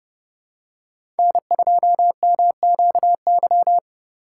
Morse code, heard as a keyed tone, about 30 wpm, 700 Hz. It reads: N2MQY
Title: USA Call Signs - 5 Characters - 2x Character Spacing - 30wpm